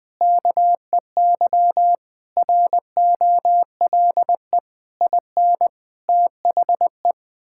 Morse code, heard as a keyed tone, about 20 wpm, 700 Hz. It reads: KEY ROLE IN THE
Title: Sets of 6 Words - 20wpm